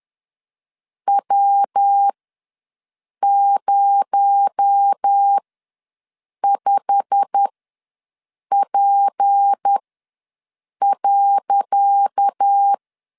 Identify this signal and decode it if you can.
Morse code, heard as a keyed tone, about 10 wpm, spaced as for 6 wpm, 800 Hz. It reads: W05P.
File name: LEZIONI / 10 SIGNS / 10C DOT ws 10wpm 4213